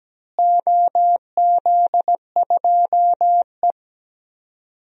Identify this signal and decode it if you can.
Morse code, heard as a keyed tone, about 17 wpm, 700 Hz. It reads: OZ2E
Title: DXCC-100 Call Signs - 4 Characters - Repeated 2x - 17wpm